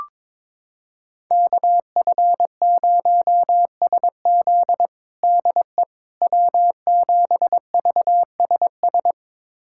Morse code, heard as a keyed tone, about 22 wpm, 700 Hz. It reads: KF0SZ DE W7VSS
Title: USA Callsign DE Callsign - 5 Characters - 22wpm